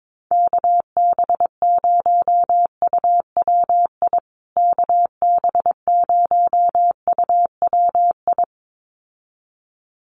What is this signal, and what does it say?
Morse code, heard as a keyed tone, about 22 wpm, 700 Hz. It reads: KB0UWI KB0UWI
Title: USA Call Signs Repeated 2x 22wpm